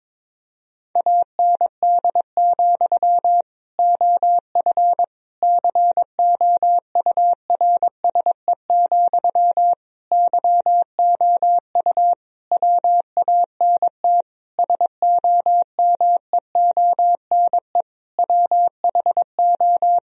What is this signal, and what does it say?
Morse code, heard as a keyed tone, about 22 wpm, 700 Hz. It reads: AND, OF COURSE, YOU WANT SOMEONE WHO
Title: Sentences from Top 1000 Words - Review Part 1 - 22wpm